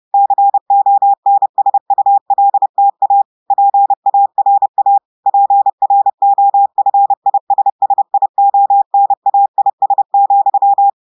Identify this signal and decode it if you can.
Morse code, heard as a keyed tone, about 30 wpm, 800 Hz. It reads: CONSULTA PARA PROFISSIONAIS,
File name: audios / portugais / portugais texte en clair 1 vitesse 30 mots minutes